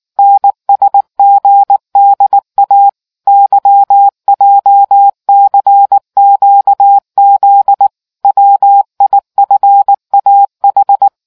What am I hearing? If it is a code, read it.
Morse code, heard as a keyed tone, about 19 wpm, 800 Hz. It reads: NSGDA YJCQZ WIFAH